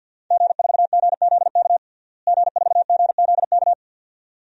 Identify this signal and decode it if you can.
Morse code, heard as a keyed tone, about 47 wpm, 700 Hz. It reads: G4GZK G4GZK
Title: DXCC-100 Call Signs - 5 Characters - Repeated 2x - 50wpm